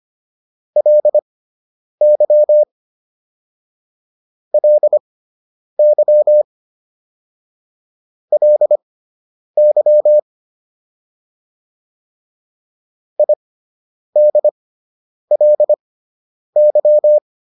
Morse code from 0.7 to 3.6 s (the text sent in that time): LY